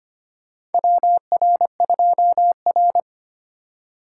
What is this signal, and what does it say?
Morse code, heard as a keyed tone, about 25 wpm, 700 Hz. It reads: WR2R
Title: USA Call Signs - 4 Characters - 3x Character Spacing - 25wpm